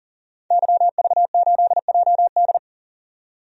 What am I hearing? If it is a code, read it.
Morse code, heard as a keyed tone, about 39 wpm, 700 Hz. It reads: YV8JD
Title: DXCC-100 Call Signs - 5 Characters - 40wpm